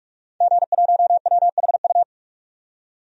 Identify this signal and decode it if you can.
Morse code, heard as a keyed tone, about 47 wpm, 700 Hz. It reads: G1WHU
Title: DXCC-100 Call Signs - 5 Characters - 45wpm